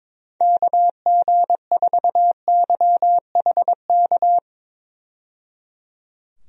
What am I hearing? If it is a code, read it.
Morse code, heard as a keyed tone, about 21 wpm, 700 Hz. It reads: KG4YHK
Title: USA Call Signs - 6 Characters - 22wpm